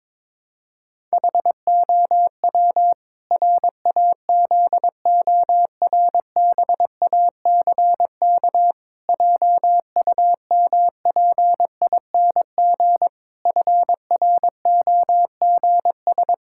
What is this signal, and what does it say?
Morse code, heard as a keyed tone, about 22 wpm, 700 Hz. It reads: HOW RAZORBACK JUMPING FROGS
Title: Pangram Challenge v2 - 22wpm